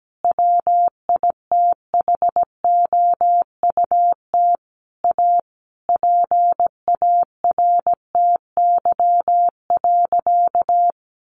Morse code, heard as a keyed tone, about 17 wpm, 700 Hz. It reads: WITHOUT A PARTY.
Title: Sentences from Top 1500 Words 17wpm